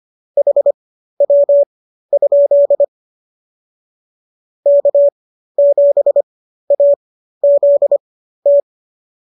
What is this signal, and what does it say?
Morse code, heard as a keyed tone, about 25 wpm, spaced as for 13 wpm, 570 Hz. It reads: HW? K7AZT